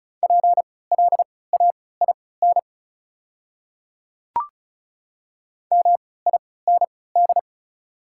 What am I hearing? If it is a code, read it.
Morse code, heard as a keyed tone, about 36 wpm, spaced as for 20 wpm, 700 Hz. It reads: PLAIN MIND
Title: 3x Character Spacing - Top 500 Words - 35wpm